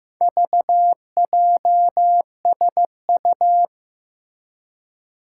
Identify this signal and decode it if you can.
Morse code, heard as a keyed tone, about 15 wpm, 700 Hz. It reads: VJSU
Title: Four Challenging Letter-Number Combinations 15wpm